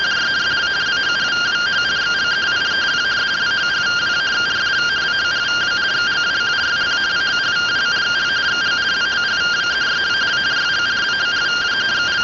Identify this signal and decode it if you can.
Two-tone teleprinter signal, about 170 Hz shift.